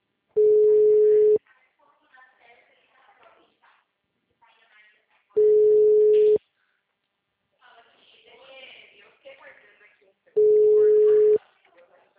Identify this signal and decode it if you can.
Ringback tone (Europe, 425 Hz), with speech.
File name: out- -2020- - - .1603